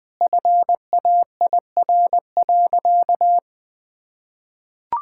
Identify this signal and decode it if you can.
Morse code, heard as a keyed tone, about 21 wpm, 700 Hz. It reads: FAIR.
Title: Sentences with 3-Word Phrases - 20wpm